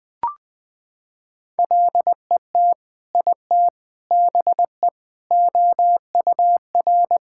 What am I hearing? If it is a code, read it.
Morse code, heard as a keyed tone, about 20 wpm, 700 Hz. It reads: LET IT BE OUR
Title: Sentences from Top 1600 Words 20wpm